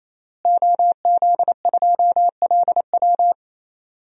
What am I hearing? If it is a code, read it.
Morse code, heard as a keyed tone, about 27 wpm, 700 Hz. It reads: OZ2LW
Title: DXCC-100 Call Signs - 5 Characters - Repeated 3x - 28wpm